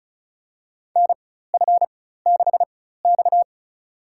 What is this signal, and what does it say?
Morse code, heard as a keyed tone, about 36 wpm, spaced as for 16 wpm, 700 Hz. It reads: NF6X
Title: USA Call Signs - 4 Characters - 4x Character Spacing - 35wpm